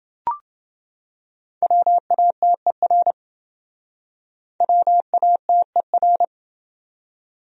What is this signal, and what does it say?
Morse code, heard as a keyed tone, about 28 wpm, 700 Hz. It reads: WATER WATER